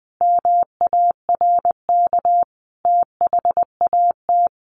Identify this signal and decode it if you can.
Morse code, heard as a keyed tone, about 20 wpm, 700 Hz. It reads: MARK THAT
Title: Sentences from Top 1400 Words 20wpm